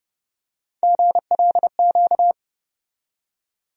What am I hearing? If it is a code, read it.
Morse code, heard as a keyed tone, about 30 wpm, 700 Hz. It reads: GLQ
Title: USA Call Sign Postfixes - 3 Letters 30wpm